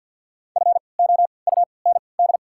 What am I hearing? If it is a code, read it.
Morse code, heard as a keyed tone, about 52 wpm, spaced as for 29 wpm, 690 Hz. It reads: FOUND
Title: 3x Character Spacing - Top 300 Words - 50wpm